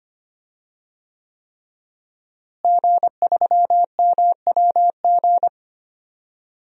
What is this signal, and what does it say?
Morse code, heard as a keyed tone, about 25 wpm, 700 Hz. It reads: G3MWG
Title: DXCC-100 Call Signs - 5 Characters - Repeated 3x - 25wpm